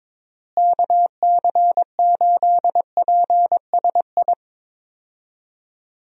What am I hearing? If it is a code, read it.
Morse code, heard as a keyed tone, about 22 wpm, 700 Hz. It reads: KC8PSI
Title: USA Call Signs - 6 Characters - Repeated 3x - 22wpm